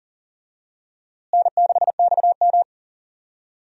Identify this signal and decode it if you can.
Morse code, heard as a keyed tone, about 39 wpm, 700 Hz. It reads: N6XM
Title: USA Call Signs - 4 Characters - Repeated 2x - 40wpm